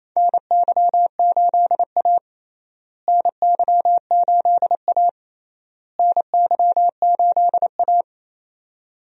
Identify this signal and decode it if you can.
Morse code, heard as a keyed tone, about 28 wpm, 700 Hz. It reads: NY8A NY8A NY8A
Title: USA Call Signs - 4 Characters - Repeated 3x - 28wpm